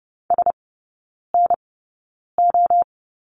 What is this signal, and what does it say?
Morse code, heard as a keyed tone, about 30 wpm, spaced as for 9 wpm, 700 Hz. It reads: SNO
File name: 081 Lesson 14e - Number 2 - Three Characters 30wpm